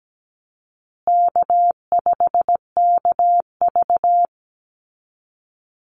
Morse code, heard as a keyed tone, about 17 wpm, 700 Hz. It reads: K5KV
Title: Call Signs Repeated 2x - 17wpm